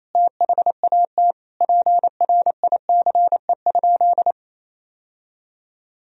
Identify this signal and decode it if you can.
Morse code, heard as a keyed tone, about 28 wpm, 700 Hz. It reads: THAT PRICE?
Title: Sentences from Top 1000 Words - Review Part 3 - 28wpm